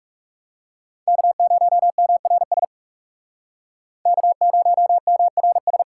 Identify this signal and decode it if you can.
Morse code, heard as a keyed tone, about 40 wpm, 700 Hz. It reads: K0MRS K0MRS